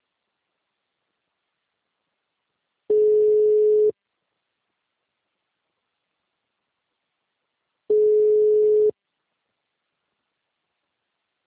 Ringback tone (Europe, 425 Hz).
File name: out- -2003- - - .9167